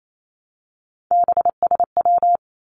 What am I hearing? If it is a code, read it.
Morse code, heard as a keyed tone, about 28 wpm, 700 Hz. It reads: BSW
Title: Three Challenging Letter-Number Combinations 28wpm